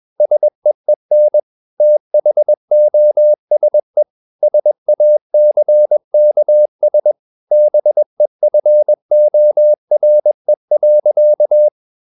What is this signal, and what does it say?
Morse code, heard as a keyed tone, about 21 wpm, 600 Hz. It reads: SEEN THOSE SACKS BEFORE.
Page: CW med Gnister